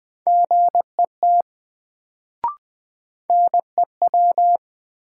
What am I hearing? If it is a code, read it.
Morse code, heard as a keyed tone, about 20 wpm, 700 Hz. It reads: GET NEW
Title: Sets of 2 Words - New Word Drill - 20wpm